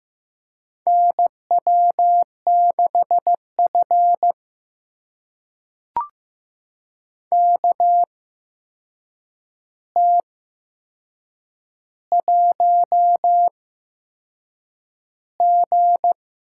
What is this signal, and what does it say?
Morse code, heard as a keyed tone, about 15 wpm, spaced as for 6 wpm, 700 Hz. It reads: NW6F KT1G
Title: USA Call Signs - 4 Characters - 8x Character Spacing - 15wpm